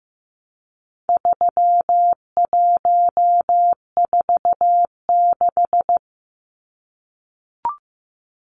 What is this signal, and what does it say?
Morse code, heard as a keyed tone, about 15 wpm, 700 Hz. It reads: 3146
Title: Numbers - 4-Digit 15wpm